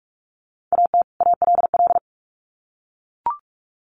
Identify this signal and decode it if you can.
Morse code, heard as a keyed tone, about 44 wpm, 700 Hz. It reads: AT ALL
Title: Sets of 2 Words 45wpm